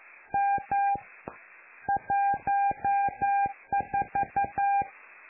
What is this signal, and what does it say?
Morse code, heard as a keyed tone, about 11 wpm, 800 Hz. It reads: M 14
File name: lu4aao / Globo Amsat